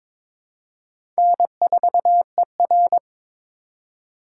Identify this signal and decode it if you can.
Morse code, heard as a keyed tone, about 22 wpm, 700 Hz. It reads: N4ER
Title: USA Call Signs - 4 Characters - 2x Character Spacing - 22wpm